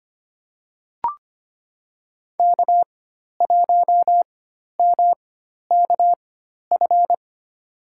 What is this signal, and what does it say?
Morse code, heard as a keyed tone, about 25 wpm, spaced as for 12 wpm, 700 Hz. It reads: K1MKF
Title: USA Call Signs - 4x Character Spacing - 25wpm